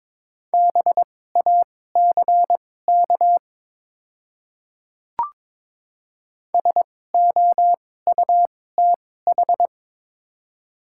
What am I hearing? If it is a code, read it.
Morse code, heard as a keyed tone, about 22 wpm, spaced as for 16 wpm, 700 Hz. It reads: BACK SOUTH
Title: 2x Character Spacing - Top 500 Words - 22wpm